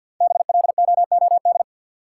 Morse code, heard as a keyed tone, about 53 wpm, 700 Hz. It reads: BLOOD
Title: Sentences from Top 1000 Words - Review Part 2 - New Word Drill - 50wpm